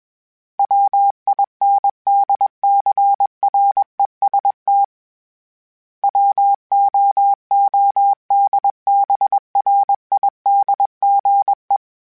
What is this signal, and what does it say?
Morse code, heard as a keyed tone, about 21 wpm, 800 Hz. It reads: WINDCREST WOODBRIDGE
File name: fromSMs / FromSMs mp3s / Cities@20